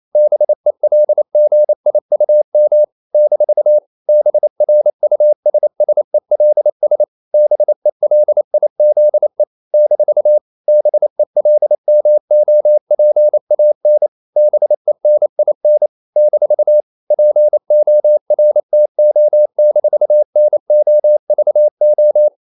Morse code, heard as a keyed tone, about 28 wpm, 600 Hz. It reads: BELGIUM - BRUSSELS BELIZE - BELMOPAN BENIN - PORTO-NOVO
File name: CW / Practice / NationalCapitals28